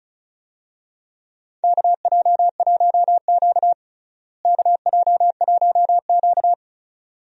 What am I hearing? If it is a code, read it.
Morse code, heard as a keyed tone, about 34 wpm, 700 Hz. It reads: KJ1Q KJ1Q
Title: USA Call Signs - 4 Characters - Repeated 2x - 35wpm